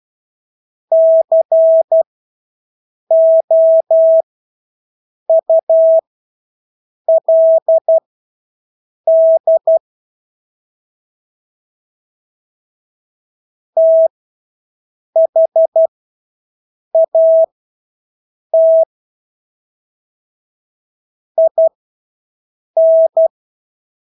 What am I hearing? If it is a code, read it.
Morse code, heard as a keyed tone, about 12 wpm, spaced as for 6 wpm, 650 Hz. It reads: COULD THAT IN
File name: Engelske ord 12-6 wpm